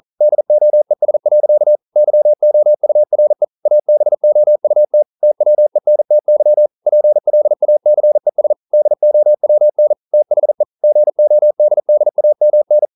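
Morse code, heard as a keyed tone, about 41 wpm, 600 Hz. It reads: DOES. YOURE ABOUT TWENTY PLACES DOWN THE GODDAMN
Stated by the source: Morse code practice at 41 wpm, from CW med Gnister.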